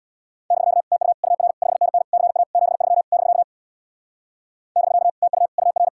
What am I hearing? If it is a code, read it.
Morse code, heard as a keyed tone, about 64 wpm, 700 Hz. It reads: <BK> TU UR 5NN CA 73 <BK> <BK> TU UR